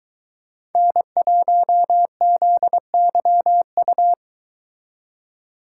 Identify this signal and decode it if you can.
Morse code, heard as a keyed tone, about 23 wpm, 700 Hz. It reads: N1ZYU